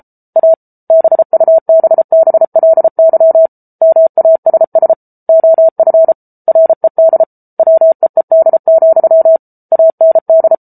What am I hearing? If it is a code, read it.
Morse code, heard as a keyed tone, about 34 wpm, 650 Hz. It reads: A BUBBLY MASS OF RED WEED, AND